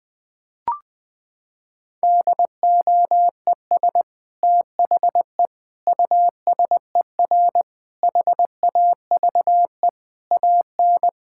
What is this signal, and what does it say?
Morse code, heard as a keyed tone, about 20 wpm, 700 Hz. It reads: DOES THE USER HAVE AN
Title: Sentences from Top 1000 Words - Review Part 5 - 20wpm